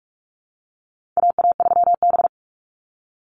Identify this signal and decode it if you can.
Morse code, heard as a keyed tone, about 44 wpm, 700 Hz. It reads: AA3B
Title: Call Signs Repeated 3x - 45wpm